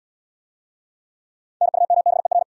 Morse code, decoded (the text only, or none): NUMBER